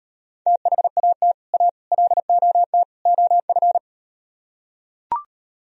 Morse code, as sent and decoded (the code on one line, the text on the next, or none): - .... .- - / .- / .-.. --- - / --- ..-.
THAT A LOT OF